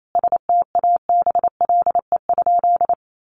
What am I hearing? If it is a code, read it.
Morse code, heard as a keyed tone, about 28 wpm, 700 Hz. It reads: STABLE?
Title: Sentences from Top 1500 Words 28wpm